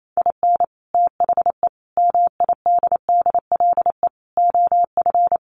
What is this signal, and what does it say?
Morse code, heard as a keyed tone, about 28 wpm, 700 Hz. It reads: IN THE MIDDLE OF
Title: Sets of 5 Words 28wpm